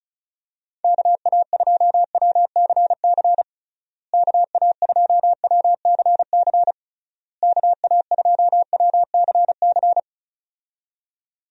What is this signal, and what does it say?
Morse code, heard as a keyed tone, about 36 wpm, 700 Hz. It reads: KA2WCC KA2WCC KA2WCC
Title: USA Call Signs - 6 Characters - Repeated 3x - 35wpm